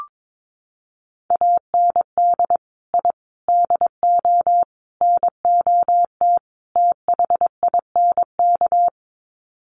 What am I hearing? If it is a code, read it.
Morse code, heard as a keyed tone, about 22 wpm, 700 Hz. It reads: AND I DO NOT THINK
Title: Sets of 5 Words 22wpm